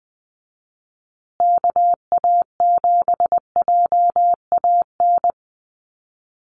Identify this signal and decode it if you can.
Morse code, heard as a keyed tone, about 20 wpm, 700 Hz. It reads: KA7JAN
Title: USA Call Signs Repeated 2x 20wpm